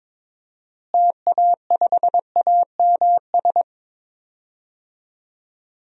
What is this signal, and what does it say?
Morse code, heard as a keyed tone, about 22 wpm, 700 Hz. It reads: TA5AMS
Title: DXCC-100 Call Signs - 6 Characters - Repeated 3x - 22wpm